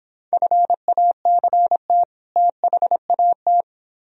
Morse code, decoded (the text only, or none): FACT THAT